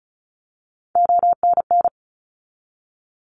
Morse code, heard as a keyed tone, about 36 wpm, 700 Hz. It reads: ONN
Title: ARRL Sections 35wpm